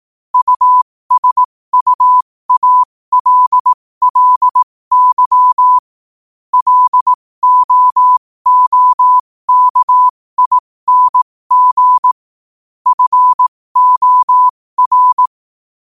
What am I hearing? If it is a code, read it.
Morse code, heard as a keyed tone, about 18 wpm, spaced as for 15 wpm, 1000 Hz. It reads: USUALLY LOOKING FOR
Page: New quotes every day in morse code at 15 Words per minute.